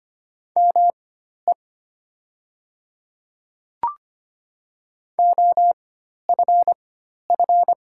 Morse code, heard as a keyed tone, about 25 wpm, spaced as for 12 wpm, 700 Hz. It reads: ME OFF
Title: 4x Character Spacing - Top 300 Words - 25wpm